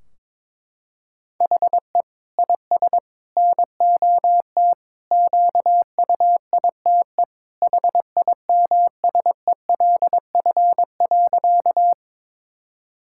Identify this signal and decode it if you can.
Morse code, heard as a keyed tone, about 22 wpm, 700 Hz. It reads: HE IS NOT QUITE HIMSELF.
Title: Sentences from Top 1600 Words 22wpm